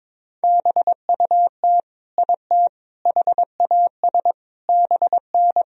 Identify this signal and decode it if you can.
Morse code, heard as a keyed tone, about 22 wpm, 700 Hz. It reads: BUT IT HAS BN
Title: Sentences from Top 100 Words and Abbreviations 22wpm